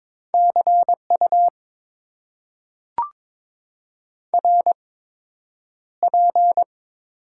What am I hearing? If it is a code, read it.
Morse code, heard as a keyed tone, about 23 wpm, spaced as for 9 wpm, 700 Hz. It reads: CU RP
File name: Top 90 QSO Elements - 8x Character Spacing - 22wpm